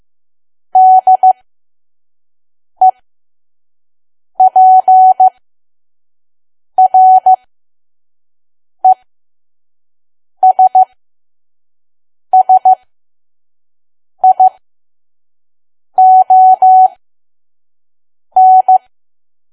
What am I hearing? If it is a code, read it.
Morse code, heard as a keyed tone, about 15 wpm, spaced as for 5 wpm, 750 Hz. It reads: DEPRESSION
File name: Archive / 5 WPM files / PM